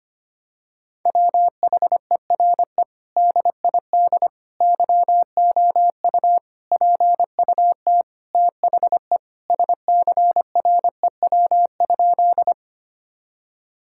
Morse code, heard as a keyed tone, about 25 wpm, 700 Hz. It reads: WHERE DID YOU PUT THE SCREW?
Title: Sentences from Top 1600 Words 25wpm